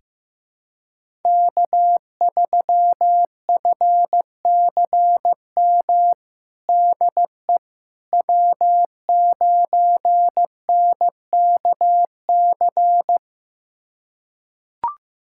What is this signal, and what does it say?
Morse code, heard as a keyed tone, about 15 wpm, 700 Hz. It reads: K3FCM DE W9NKC
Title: USA Callsign DE Callsign - 5 Characters - 15wpm